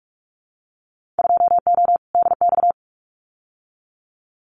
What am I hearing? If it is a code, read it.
Morse code, heard as a keyed tone, about 44 wpm, 700 Hz. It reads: 2O DX
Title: Winter Field Day Exchanges Encore 45wpm